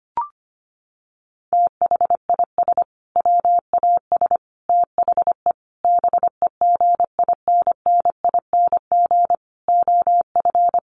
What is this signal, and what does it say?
Morse code, heard as a keyed tone, about 25 wpm, 700 Hz. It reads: THIS WAS THE BEGINNING OF
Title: Sentences from Top 900 Words 25wpm